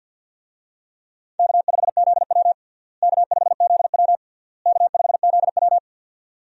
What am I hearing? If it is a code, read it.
Morse code, heard as a keyed tone, about 51 wpm, 700 Hz. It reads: K5ZW K5ZW K5ZW
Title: USA Call Signs - 4 Characters - Repeated 3x - 50wpm